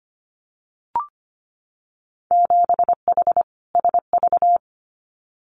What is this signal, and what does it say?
Morse code, heard as a keyed tone, about 25 wpm, 700 Hz. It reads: 7H SV
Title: Winter Field Day Exchanges Encore 25wpm